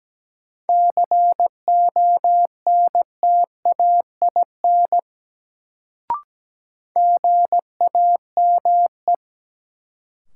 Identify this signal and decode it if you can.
Morse code, heard as a keyed tone, about 17 wpm, 700 Hz. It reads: CONTAIN GAME
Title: Sentences from Top 500 Words - New Word Drill - 17wpm